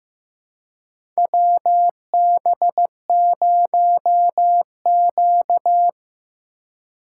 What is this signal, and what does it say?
Morse code, heard as a keyed tone, about 15 wpm, 700 Hz. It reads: WB0Q
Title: USA Call Signs - 4 Characters - 2x Character Spacing - 15wpm